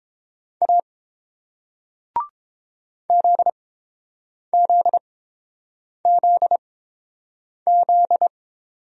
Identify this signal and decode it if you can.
Morse code, heard as a keyed tone, about 31 wpm, spaced as for 7 wpm, 690 Hz. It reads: A ZZZZ